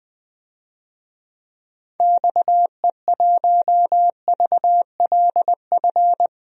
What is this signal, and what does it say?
Morse code, heard as a keyed tone, about 20 wpm, 700 Hz. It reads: XE1VLF
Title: DXCC-100 Call Signs - 6 Characters - 20wpm